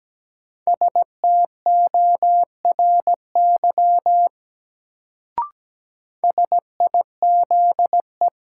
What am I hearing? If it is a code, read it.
Morse code, heard as a keyed tone, about 17 wpm, 700 Hz. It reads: STORY SIZE
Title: Sets of 3 Words - New Word Drill - 17wpm